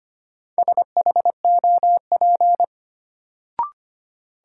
Morse code, heard as a keyed tone, about 25 wpm, 700 Hz. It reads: SHOP
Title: Sentences from Top 1600 Words - New Word Drill Alt - 25wpm